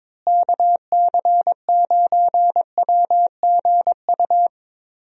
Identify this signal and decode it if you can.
Morse code, heard as a keyed tone, about 22 wpm, 700 Hz. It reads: KC9WGU
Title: USA Call Signs - 6 Characters - Repeated 3x - 22wpm